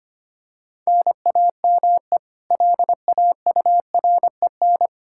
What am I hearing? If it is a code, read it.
Morse code, heard as a keyed tone, about 25 wpm, 700 Hz. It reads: NAME LAUREN
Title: Practice QSO - Exchange 1 - Beginning - 25wpm